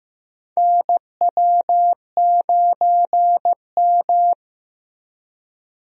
Morse code, heard as a keyed tone, about 15 wpm, 700 Hz. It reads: NW9M
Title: USA Call Signs - 4 Characters - Repeated 3x - 15wpm